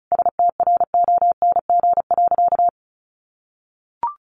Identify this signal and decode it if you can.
Morse code, heard as a keyed tone, about 34 wpm, 700 Hz. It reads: STRONG.
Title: Sentences from Top 1400 Words 35wpm